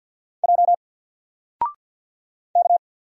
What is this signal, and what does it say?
Morse code, heard as a keyed tone, about 50 wpm, 700 Hz. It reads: J K
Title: Single Letter-Number 50wpm